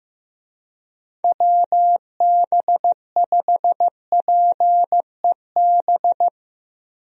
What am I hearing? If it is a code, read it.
Morse code, heard as a keyed tone, about 16 wpm, 700 Hz. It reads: WB5PEB
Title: USA Call Signs - 6 Characters - 15wpm